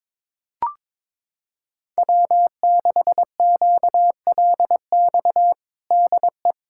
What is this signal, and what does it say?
Morse code, heard as a keyed tone, about 22 wpm, 700 Hz. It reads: W6QLX DE
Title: USA Callsign DE Callsign - 5 Characters - 22wpm